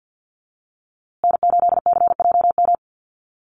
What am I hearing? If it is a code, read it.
Morse code, heard as a keyed tone, about 50 wpm, 700 Hz. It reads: N8CJM
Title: USA Call Signs Repeated 3x 50wpm